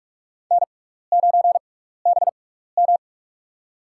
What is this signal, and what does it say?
Morse code, heard as a keyed tone, about 46 wpm, spaced as for 16 wpm, 700 Hz. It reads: N9BM
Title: USA Call Signs - 4 Characters - 6x Character Spacing - 45wpm